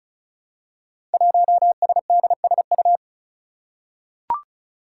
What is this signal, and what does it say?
Morse code, heard as a keyed tone, about 36 wpm, 700 Hz. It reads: 1SDSU
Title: Five Challenging Letter-Number Combinations 35wpm